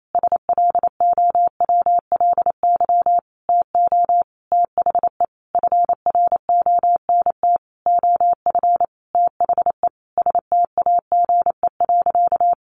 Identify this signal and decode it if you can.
Morse code, heard as a keyed tone, about 28 wpm, 700 Hz. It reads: SLOWLY TO THE FRONT OF THE STAGE.
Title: Sentences from Top 1400 Words 28wpm